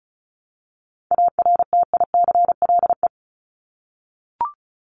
Morse code, heard as a keyed tone, about 34 wpm, 700 Hz. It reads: ARTICLE
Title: Top 751-1000 Words 35wpm